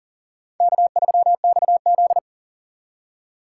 Morse code, heard as a keyed tone, about 37 wpm, 700 Hz. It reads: K3XZ
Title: USA Call Signs - 4 Characters - 6x Character Spacing - 40wpm